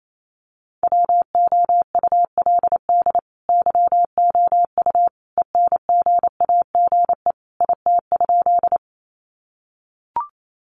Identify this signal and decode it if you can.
Morse code, heard as a keyed tone, about 28 wpm, 700 Hz. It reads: WOULD YOU ENGAGE IT?
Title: Sentences from Top 1500 Words 28wpm